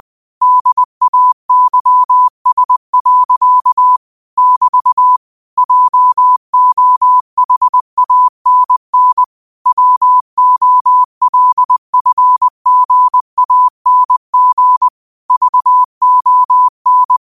Morse code, heard as a keyed tone, about 20 wpm, 1000 Hz. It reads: DAYS. <BT> JOHANN WOLFGANG VON
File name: qotdfiles / 20wpm